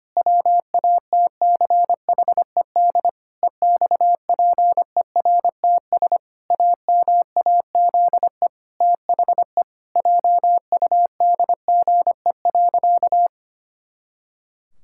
Morse code, heard as a keyed tone, about 25 wpm, 700 Hz. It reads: WATCHED EXPERTS AMAZE THE JUDGE.
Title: Pangram Challenge v2 - 25wpm